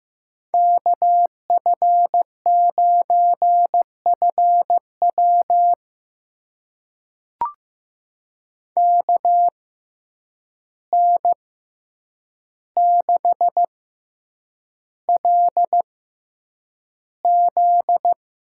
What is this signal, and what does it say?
Morse code, heard as a keyed tone, about 15 wpm, spaced as for 8 wpm, 700 Hz. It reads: KF9FW KN6LZ
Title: USA Call Signs - 5 Characters - 6x Character Spacing - 15wpm